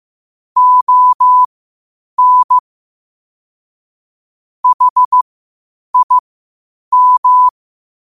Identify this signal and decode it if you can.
Morse code, heard as a keyed tone, about 15 wpm, spaced as for 8 wpm, 1000 Hz. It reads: ON HIM
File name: qotdfiles / 8wpm